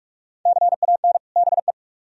Morse code, heard as a keyed tone, about 40 wpm, 700 Hz. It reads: CAN BE